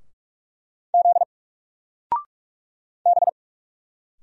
Morse code, heard as a keyed tone, about 45 wpm, 700 Hz. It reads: Z B
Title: Single Letter-Number 45wpm